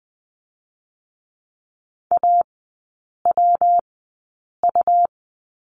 Morse code, heard as a keyed tone, about 21 wpm, spaced as for 8 wpm, 690 Hz. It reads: AWU
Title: 159 Lesson 26e - Letter g - Three Characters 20wpm